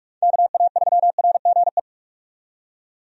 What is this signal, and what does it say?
Morse code, heard as a keyed tone, about 46 wpm, 700 Hz. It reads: KA3RGE
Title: USA Call Signs - 6 Characters - 45wpm